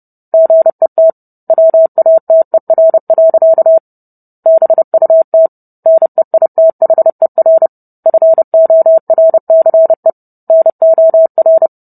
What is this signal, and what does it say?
Morse code, heard as a keyed tone, about 30 wpm, 650 Hz. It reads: GET WATER. BUT NEITHER FORCE NOR